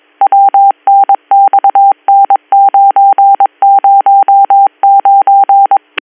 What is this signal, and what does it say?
Morse code, heard as a keyed tone, about 22 wpm, 790 Hz. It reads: WNXN909
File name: 202- .0-call 7115